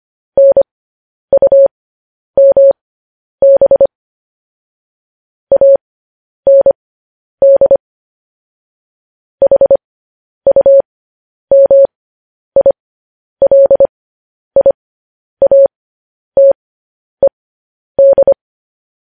Morse code, heard as a keyed tone, about 25 wpm, spaced as for 10 wpm, 550 Hz. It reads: NUMB AND HUMILIATED
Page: news-headlines-in-morse-code-at-10-wpm